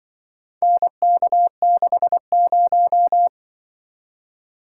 Morse code, heard as a keyed tone, about 25 wpm, 700 Hz. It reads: NK60